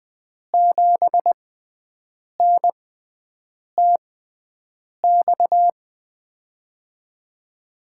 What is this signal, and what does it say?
Morse code, heard as a keyed tone, about 20 wpm, spaced as for 7 wpm, 700 Hz. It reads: 7NTX
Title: USA Call Signs - 6 Characters - 6x Character Spacing - 20wpm